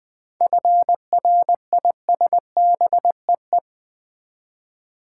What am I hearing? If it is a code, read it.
Morse code, heard as a keyed tone, about 20 wpm, 700 Hz. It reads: FRISBEE
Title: Easy Words with Hints - 20wpm